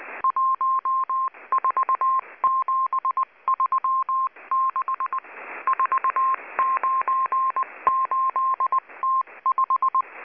Morse code, heard as a keyed tone, about 20 wpm, 1060 Hz. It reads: 14736 498T5
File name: HM02- -0540utc